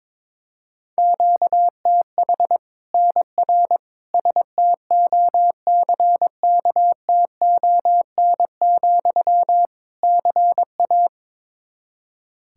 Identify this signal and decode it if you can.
Morse code, heard as a keyed tone, about 22 wpm, 700 Hz. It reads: QTH NR STOCKTON, CA
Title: Practice QSO - Exchange 1 - Beginning - 22wpm